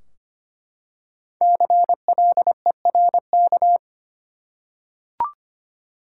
Morse code, heard as a keyed tone, about 25 wpm, 700 Hz. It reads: CLERK
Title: Five Letter Words - 25wpm